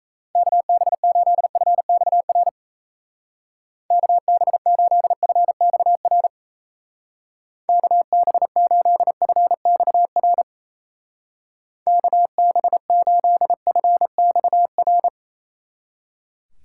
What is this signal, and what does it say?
Morse code, heard as a keyed tone, about 34 wpm, 700 Hz. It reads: KB8FXR KB8FXR KB8FXR KB8FXR
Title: Speed Racing - USA 2x3 Call Signs 28wpm